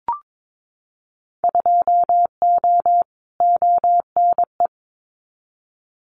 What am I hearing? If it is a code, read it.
Morse code, heard as a keyed tone, about 22 wpm, 700 Hz. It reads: 2O ONE
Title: Winter Field Day Exchanges Encore 22wpm